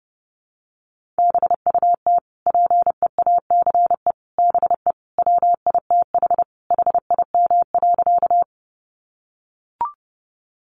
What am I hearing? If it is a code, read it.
Morse code, heard as a keyed tone, about 28 wpm, 700 Hz. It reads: BUT PEACE BE WITH HIM.
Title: Sentences from Top 1500 Words 30wpm